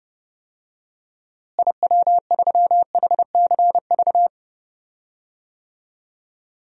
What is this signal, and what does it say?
Morse code, heard as a keyed tone, about 32 wpm, 700 Hz. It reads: IW3HCV
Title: DXCC-100 Call Signs - 6 Characters - Repeated 2x - 30wpm